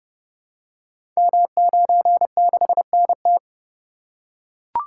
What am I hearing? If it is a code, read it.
Morse code, heard as a keyed tone, about 29 wpm, 700 Hz. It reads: M96NT
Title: Code Groups - 5 Letter:Numbers - 30wpm